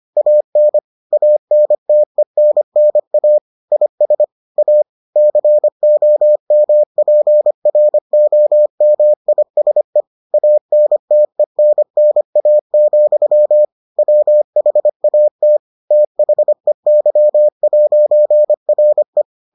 Morse code, heard as a keyed tone, about 25 wpm, 600 Hz. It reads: AN ANTENNA IS A COMPROMISE ANTENNA, WHAT THEY'RE